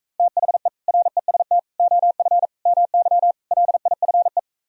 Morse code, heard as a keyed tone, about 45 wpm, 700 Hz. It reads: THE REST OF MY LIFE